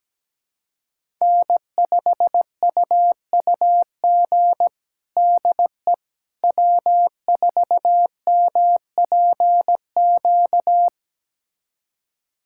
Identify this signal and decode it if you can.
Morse code, heard as a keyed tone, about 17 wpm, 700 Hz. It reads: N5UUG DE W4MPQ
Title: USA Callsign DE Callsign - 5 Characters - 17wpm